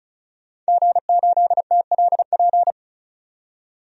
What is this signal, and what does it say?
Morse code, heard as a keyed tone, about 34 wpm, 700 Hz. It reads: G8TLP
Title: DXCC-100 Call Signs - 5 Characters - 35wpm